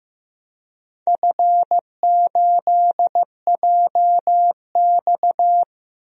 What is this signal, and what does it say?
Morse code, heard as a keyed tone, about 15 wpm, 700 Hz. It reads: F8JX
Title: DXCC-100 Call Signs - 4 Characters - Repeated 3x - 15wpm